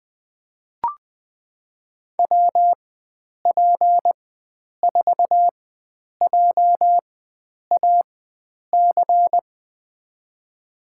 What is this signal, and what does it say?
Morse code, heard as a keyed tone, about 21 wpm, spaced as for 9 wpm, 700 Hz. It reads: WP4JAC
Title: USA Call Signs - 6 Characters - 4x Character Spacing - 20wpm